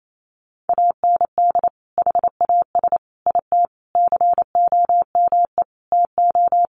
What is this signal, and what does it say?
Morse code, heard as a keyed tone, about 28 wpm, 700 Hz. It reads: AND HAS IT COME TO
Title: Sentences from Top 100 Words 28wpm